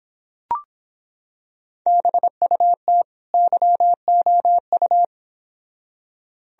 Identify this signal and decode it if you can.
Morse code, heard as a keyed tone, about 25 wpm, 700 Hz. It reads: BUT YOU